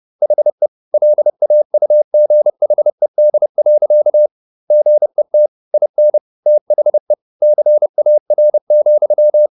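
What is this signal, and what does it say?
Morse code, heard as a keyed tone, about 30 wpm, 600 Hz. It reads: HE LAUGHED. GET IN THE CAR,
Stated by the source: Morse code at 30 wpm, the novel Never by Ken Follett in English, from CW med Gnister.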